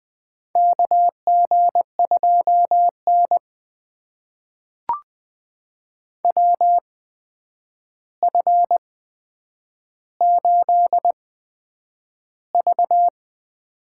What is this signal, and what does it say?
Morse code, heard as a keyed tone, about 20 wpm, spaced as for 9 wpm, 700 Hz. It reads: KG2N WF8V
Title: USA Call Signs - 4 Characters - 8x Character Spacing - 20wpm